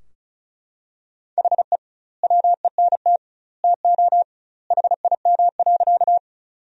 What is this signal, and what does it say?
Morse code, heard as a keyed tone, about 34 wpm, 700 Hz. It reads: HE WENT TO HIM.
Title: 2x Word-Spacing - Sentences from Top 200 Words - 35wpm